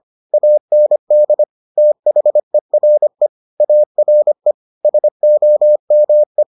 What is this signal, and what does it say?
Morse code, heard as a keyed tone, about 25 wpm, 600 Hz. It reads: AND THERE ARE SOME